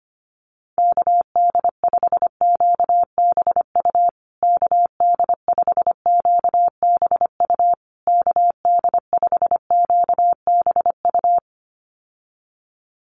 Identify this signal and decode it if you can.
Morse code, heard as a keyed tone, about 25 wpm, 700 Hz. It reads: KD5QBU KD5QBU KD5QBU
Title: USA Call Signs Repeated 3x 25wpm